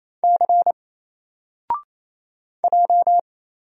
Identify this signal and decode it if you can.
Morse code, heard as a keyed tone, about 27 wpm, 700 Hz. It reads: C J
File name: Single-Letter-Number-28wpm